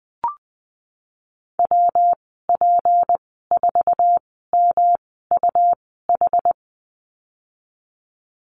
Morse code, heard as a keyed tone, about 19 wpm, spaced as for 14 wpm, 700 Hz. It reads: WP4MUH
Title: 161 Lesson 26g - Letter g - Call Signs 20wpm